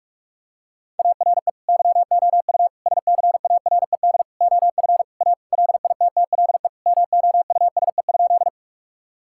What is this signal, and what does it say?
Morse code, heard as a keyed tone, about 46 wpm, 700 Hz. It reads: ARE YOU SCARED OF A LITTLE MOUSE?
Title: Sentences from Top 1000 Words - Review Part 5 - 45wpm